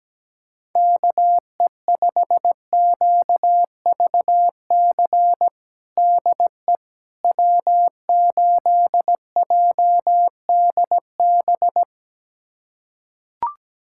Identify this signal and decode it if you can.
Morse code, heard as a keyed tone, about 17 wpm, 700 Hz. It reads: KE5QVC DE W8JDB
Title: USA Callsign DE Callsign 17wpm